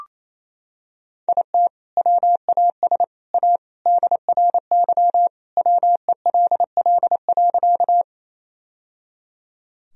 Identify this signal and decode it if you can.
Morse code, heard as a keyed tone, about 28 wpm, 700 Hz. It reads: IT WAS A DRY WELL.
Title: Sentences from Top 1000 Words - Review Part 3 - 28wpm